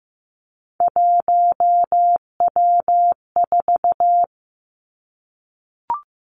Morse code, heard as a keyed tone, about 15 wpm, 700 Hz. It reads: 1W4
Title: Three Challenging Letter-Number Combinations 15wpm